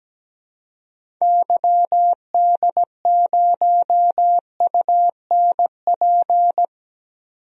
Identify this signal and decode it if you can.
Morse code, heard as a keyed tone, about 17 wpm, 700 Hz. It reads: YD0UNP
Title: DXCC-100 Call Signs - 6 Characters - 17wpm